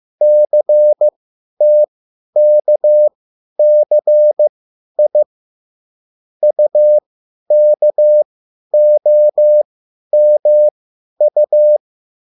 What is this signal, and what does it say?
Morse code, heard as a keyed tone, about 15 wpm, spaced as for 10 wpm, 600 Hz. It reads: CTKCI UKOMU
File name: QSL / CW lesson